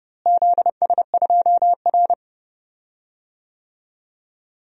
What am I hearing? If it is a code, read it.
Morse code, heard as a keyed tone, about 30 wpm, 700 Hz. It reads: ZS2R
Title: DXCC-100 Call Signs - 4 Characters - Repeated 3x - 30wpm